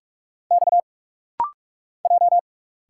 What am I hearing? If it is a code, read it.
Morse code, heard as a keyed tone, about 46 wpm, 700 Hz. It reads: X J
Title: Single Letter-Number Alt - Rapid-Fire 45wpm